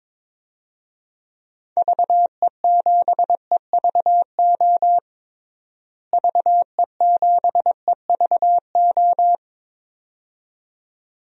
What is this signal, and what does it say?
Morse code, heard as a keyed tone, about 21 wpm, 700 Hz. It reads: VE7EVO VE7EVO
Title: DXCC-100 Call Signs - 6 Characters - Repeated 2x - 22wpm